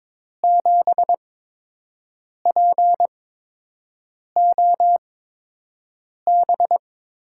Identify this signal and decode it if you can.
Morse code, heard as a keyed tone, about 21 wpm, spaced as for 6 wpm, 700 Hz. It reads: 7POB
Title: USA Call Signs - 8x Character Spacing - 22wpm